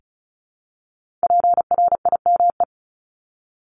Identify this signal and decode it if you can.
Morse code, heard as a keyed tone, about 36 wpm, 700 Hz. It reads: PRIME
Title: Multi-letter Phonemes Repeated 3x 35wpm